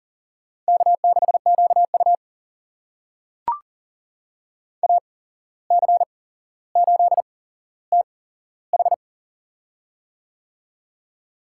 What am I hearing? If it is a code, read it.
Morse code, heard as a keyed tone, about 39 wpm, spaced as for 16 wpm, 700 Hz. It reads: K6QU AC8TH
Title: USA Call Signs - 8x Character Spacing - 40wpm